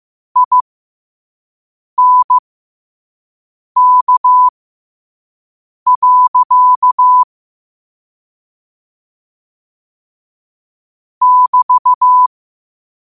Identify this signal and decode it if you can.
Morse code, heard as a keyed tone, about 15 wpm, spaced as for 5 wpm, 1000 Hz. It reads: INK. <BT>
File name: qotdfiles / 5wpm